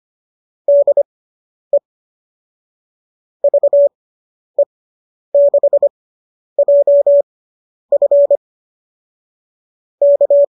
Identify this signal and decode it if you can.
Morse code, heard as a keyed tone, about 25 wpm, spaced as for 10 wpm, 570 Hz. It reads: DE VE6JF K